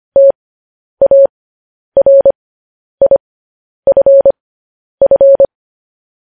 Morse code, heard as a keyed tone, about 25 wpm, spaced as for 10 wpm, 550 Hz. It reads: TARIFF
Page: news headlines in morse code at 10 wpm